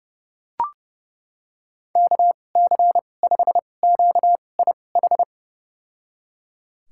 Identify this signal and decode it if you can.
Morse code, heard as a keyed tone, about 30 wpm, spaced as for 22 wpm, 700 Hz. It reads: KC5QIH
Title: USA Call Signs - 6 Characters - 2x Character Spacing - 30wpm